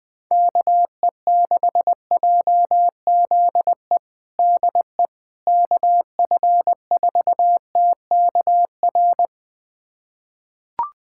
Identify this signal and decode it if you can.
Morse code, heard as a keyed tone, about 20 wpm, 700 Hz. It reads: KE6JZE DE KF4TKR
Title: USA Callsign DE Callsign - 6 Characters - 20wpm